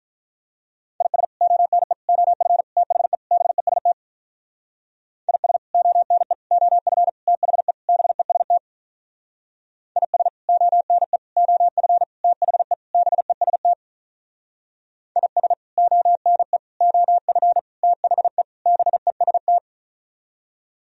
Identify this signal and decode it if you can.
Morse code, heard as a keyed tone, about 43 wpm, 700 Hz. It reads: IS ONE OF THE BEST IS ONE OF THE BEST IS ONE OF THE BEST IS ONE OF THE BEST
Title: Speed-Racing - Sets of 5 Words 35wpm